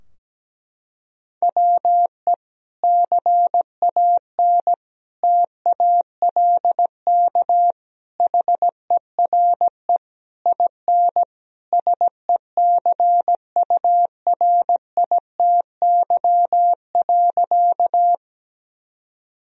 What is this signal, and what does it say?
Morse code, heard as a keyed tone, about 17 wpm, 700 Hz. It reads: WE CAN TALK HERE IN SECURITY.
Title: Sentences from Top 1000 Words - Review Part 4 - 17wpm